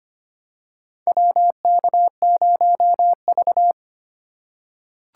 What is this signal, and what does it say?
Morse code, heard as a keyed tone, about 25 wpm, 700 Hz. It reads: WK0V
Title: USA Call Signs - 4 Characters - 2x Character Spacing - 25wpm